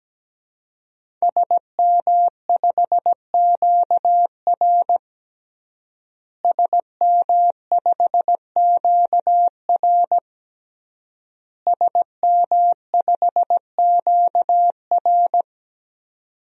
Morse code, heard as a keyed tone, about 17 wpm, 700 Hz. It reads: SM5QR SM5QR SM5QR
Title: DXCC-100 Call Signs - 5 Characters - Repeated 3x - 17wpm